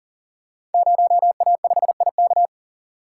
Morse code, heard as a keyed tone, about 42 wpm, 700 Hz. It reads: 0A5IK
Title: Code Groups - 5 Letter:Numbers - 40wpm